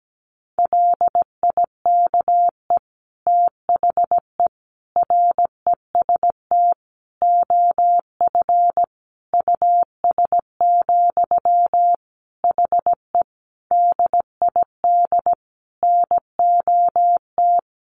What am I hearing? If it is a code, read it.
Morse code, heard as a keyed tone, about 17 wpm, 700 Hz. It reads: LIKE THE REST OF US, HE DID NOT
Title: Sentences from Top 1400 Words 17wpm